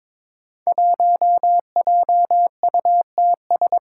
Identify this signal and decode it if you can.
Morse code, heard as a keyed tone, about 22 wpm, 700 Hz. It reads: 1JUTS
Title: Five Challenging Letter-Number Combinations 22wpm